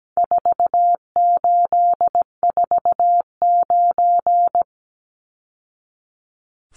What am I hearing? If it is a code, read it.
Morse code, heard as a keyed tone, about 17 wpm, 700 Hz. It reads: 4849
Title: Numbers - 4-Digit 17wpm